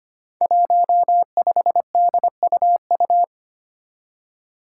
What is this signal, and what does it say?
Morse code, heard as a keyed tone, about 25 wpm, 700 Hz. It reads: 15DUU
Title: Five Challenging Letter-Number Combinations 25wpm